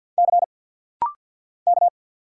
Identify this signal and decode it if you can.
Morse code, heard as a keyed tone, about 50 wpm, 700 Hz. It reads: C K
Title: Single Letter-Number Alt - Rapid-Fire 50wpm